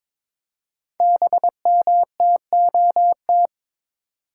Morse code, heard as a keyed tone, about 22 wpm, 690 Hz. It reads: BMTOT
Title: Code Groups - 5 Letters - 22wpm